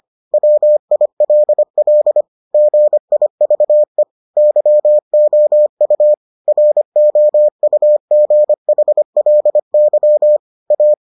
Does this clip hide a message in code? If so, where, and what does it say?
Yes: Morse code from 0.3 to 10.9 s, heard as a keyed tone, about 25 wpm, 600 Hz. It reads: WILL GIVE YOU ROUGHLY A